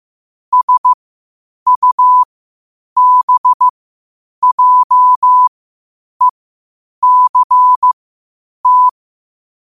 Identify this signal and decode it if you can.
Morse code, heard as a keyed tone, about 15 wpm, spaced as for 8 wpm, 1000 Hz. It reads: SUBJECT
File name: qotdfiles / 8wpm